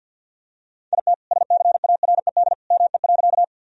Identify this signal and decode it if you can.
Morse code, heard as a keyed tone, about 50 wpm, 700 Hz. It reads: IT SCARED ME.
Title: Sentences from Top 1000 Words - Review Part 4 - 50wpm